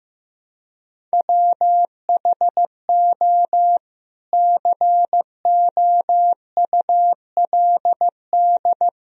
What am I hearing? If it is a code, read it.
Morse code, heard as a keyed tone, about 16 wpm, 700 Hz. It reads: WHO COULD
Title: Sentences from Top 300 Words and Abbreviations 15wpm